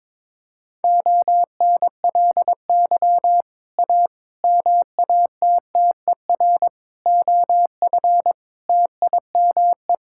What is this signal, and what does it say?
Morse code, heard as a keyed tone, about 21 wpm, 700 Hz. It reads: ONLY A MATTER OF TIME
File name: practice / 22wpm / Sets of 6 Words - 22wpm